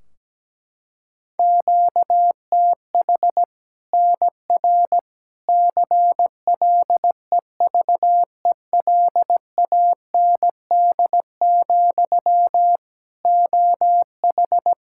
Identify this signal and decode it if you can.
Morse code, heard as a keyed tone, about 17 wpm, 700 Hz. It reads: QTH NR CLEVELAND, OH
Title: K1USN SST QSO Practice 17wpm